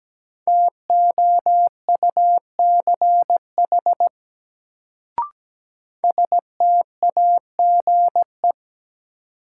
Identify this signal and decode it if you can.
Morse code, heard as a keyed tone, about 17 wpm, 700 Hz. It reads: TOUCH STAGE
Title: Sentences from Top 1200 Words - New Word Drill - 17wpm